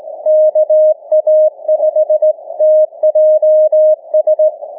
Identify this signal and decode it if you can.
Morse code, heard as a keyed tone, about 18 wpm, 620 Hz. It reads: KA5TJS
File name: cw / ssn 30M